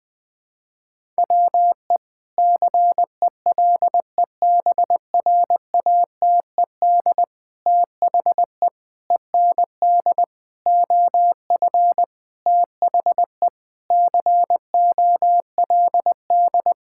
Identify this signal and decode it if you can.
Morse code, heard as a keyed tone, about 20 wpm, 700 Hz. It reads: WE CELEBRATED THE END OF THE COLD
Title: Sentences with 5-Word Phrases - Spoken Beginning - 20wpm